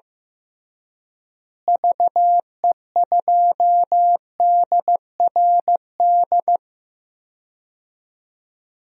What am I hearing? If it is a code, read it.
Morse code, heard as a keyed tone, about 15 wpm, 700 Hz. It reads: VE2DRD
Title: DXCC-100 Call Signs - 6 Characters - Repeated 3x - 15wpm